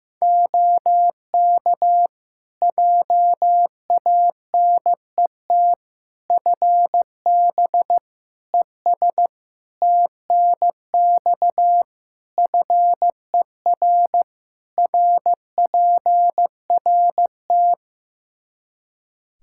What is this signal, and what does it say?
Morse code, heard as a keyed tone, about 15 wpm, 700 Hz. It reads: OK JANET FB ES TNX FER RPRT
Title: Practice QSO - Exchange 2 - Info - 15wpm